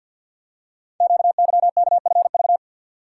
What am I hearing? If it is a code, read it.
Morse code, heard as a keyed tone, about 53 wpm, 700 Hz. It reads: QYKFV
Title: Code Groups - 5 Letters - 50wpm